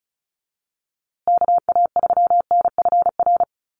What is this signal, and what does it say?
Morse code, heard as a keyed tone, about 34 wpm, 700 Hz. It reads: KA3NFR
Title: USA Call Signs Repeated 3x 35wpm